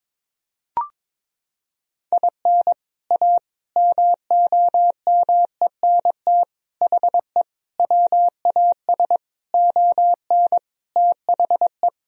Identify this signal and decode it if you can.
Morse code, heard as a keyed tone, about 21 wpm, 700 Hz. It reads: IN A MOMENT HE WAS ON THE
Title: Sentences from Top 1000 Words - Review Part 2 - 22wpm